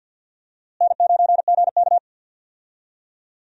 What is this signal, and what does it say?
Morse code, heard as a keyed tone, about 50 wpm, 700 Hz. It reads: N9GK
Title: USA Call Signs - 4 Characters - Repeated 3x - 50wpm